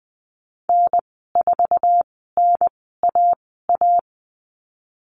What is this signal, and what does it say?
Morse code, heard as a keyed tone, about 20 wpm, spaced as for 14 wpm, 700 Hz. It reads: N4NAA
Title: 069 Lesson 12g - Letter d - Call Signs 20wpm